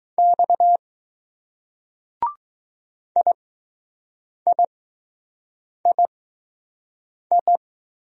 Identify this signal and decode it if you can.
Morse code, heard as a keyed tone, about 20 wpm, spaced as for 6 wpm, 700 Hz. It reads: X IIII